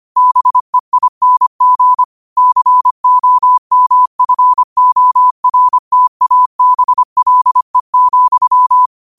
Morse code, heard as a keyed tone, about 25 wpm, 1000 Hz. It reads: BEING COMFORTABLE,
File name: qotdfiles / 25wpm